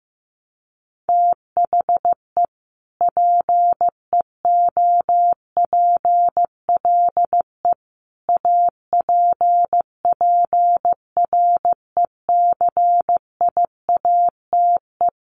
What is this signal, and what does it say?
Morse code, heard as a keyed tone, about 15 wpm, 700 Hz. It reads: THE PEOPLE APPRECIATE
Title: Sentences from Top 1300 Words 15wpm